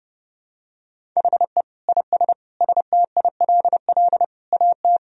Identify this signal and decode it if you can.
Morse code, heard as a keyed tone, about 30 wpm, 700 Hz. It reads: HE IS STILL AT
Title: Sentences from Top 1000 Words - Review Part 1 - 30wpm